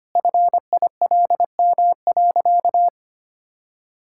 Morse code, heard as a keyed tone, about 25 wpm, 700 Hz. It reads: FILM.
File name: Easy Sentences with Hints - 25wpm